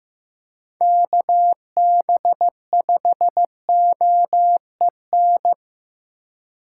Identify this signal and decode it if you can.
Morse code, heard as a keyed tone, about 15 wpm, 700 Hz. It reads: KB5OEN
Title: USA Call Signs - 6 Characters - 2x Character Spacing - 15wpm